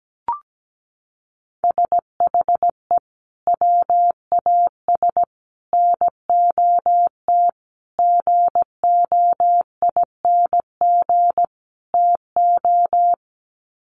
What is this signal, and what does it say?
Morse code, heard as a keyed tone, about 17 wpm, 700 Hz. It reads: SHE WAS NOT GOING TO
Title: Sets of 5 Words Encore2 17wpm